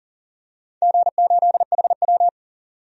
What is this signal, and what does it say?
Morse code, heard as a keyed tone, about 40 wpm, 700 Hz. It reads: G8HW
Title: DXCC-100 Call Signs - 4 Characters - 40wpm